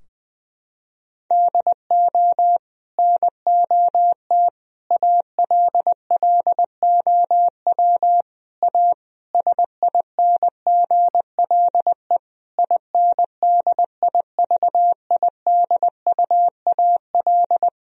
Morse code, heard as a keyed tone, about 20 wpm, 700 Hz. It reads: DO NOT ALLOW A SINGLE INDIVIDUAL
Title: Sentences from Top 1000 Words - Review Part 4 - 20wpm